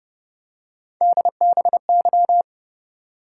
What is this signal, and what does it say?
Morse code, heard as a keyed tone, about 30 wpm, 700 Hz. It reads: DBY